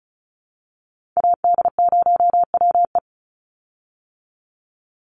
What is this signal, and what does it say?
Morse code, heard as a keyed tone, about 34 wpm, 700 Hz. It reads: AD0WE
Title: Call Signs Repeated 3x - 35wpm